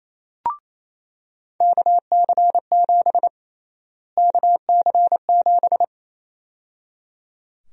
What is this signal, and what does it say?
Morse code, heard as a keyed tone, about 28 wpm, 700 Hz. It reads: KC7 KC7
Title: USA Call Sign Prefixes - Repeated 2x - 28wpm